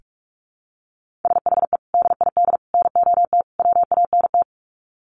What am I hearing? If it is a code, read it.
Morse code, heard as a keyed tone, about 44 wpm, 700 Hz. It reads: SHE DID NOT WANT
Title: Sets of 4 Words 45wpm